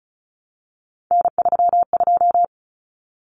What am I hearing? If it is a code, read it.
Morse code, heard as a keyed tone, about 34 wpm, 700 Hz. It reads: N32
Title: Common Cut Numbers - 3-Digit 35wpm